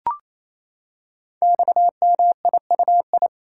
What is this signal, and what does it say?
Morse code, heard as a keyed tone, about 28 wpm, 700 Hz. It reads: XMIUI
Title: Code Groups - 5 Letters - 28wpm